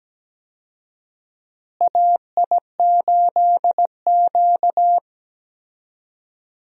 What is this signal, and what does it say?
Morse code, heard as a keyed tone, about 17 wpm, 700 Hz. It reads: AI8Q
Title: USA Call Signs - 4 Characters - Repeated 3x - 17wpm